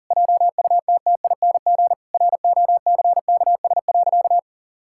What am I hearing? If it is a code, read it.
Morse code, heard as a keyed tone, about 40 wpm, 700 Hz. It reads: JUTTING ROCKS.
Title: Pangram Challenge v2 - 40wpm